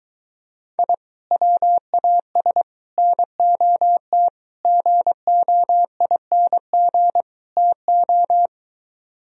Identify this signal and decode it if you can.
Morse code, heard as a keyed tone, about 23 wpm, 700 Hz. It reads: I WAS NOT GOING TO